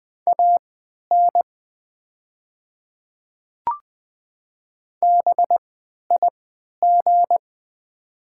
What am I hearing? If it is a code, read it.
Morse code, heard as a keyed tone, about 20 wpm, spaced as for 11 wpm, 700 Hz. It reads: AN BIG
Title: 3x Character Spacing - Top 200 Words - 20wpm